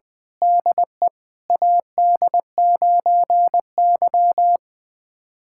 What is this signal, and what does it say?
Morse code, heard as a keyed tone, about 20 wpm, 700 Hz. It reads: DE AD9Y
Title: USA Callsign DE Callsign - 4 Characters - 20wpm